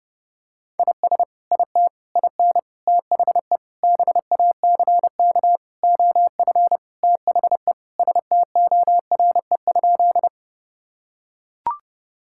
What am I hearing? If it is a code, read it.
Morse code, heard as a keyed tone, about 28 wpm, 700 Hz. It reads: IS IT IN THE BACK OF THE STORE?
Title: Sentences from Top 1000 Words - Review Part 5 - 30wpm